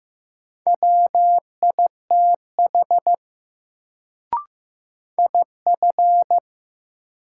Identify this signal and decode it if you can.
Morse code, heard as a keyed tone, about 15 wpm, 700 Hz. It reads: WITH IF
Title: Sets of 2 Words - New Word Drill - 15wpm